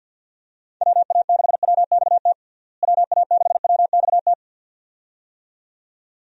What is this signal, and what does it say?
Morse code, heard as a keyed tone, about 51 wpm, 700 Hz. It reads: WA6WXT WA6WXT
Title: USA Call Signs - 6 Characters - Repeated 2x - 50wpm